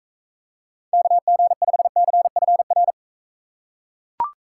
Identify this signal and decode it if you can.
Morse code, heard as a keyed tone, about 42 wpm, 700 Hz. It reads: KG5CFR